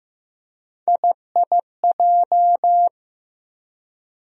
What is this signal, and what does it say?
Morse code, heard as a keyed tone, about 15 wpm, 700 Hz. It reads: IIJ
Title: USA Call Sign Postfixes - 3 Letters 15wpm